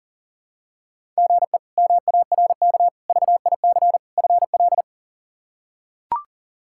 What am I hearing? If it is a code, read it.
Morse code, heard as a keyed tone, about 37 wpm, 700 Hz. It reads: GE MARK VIC FL
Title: K1USN SST QSO Practice - 40wpm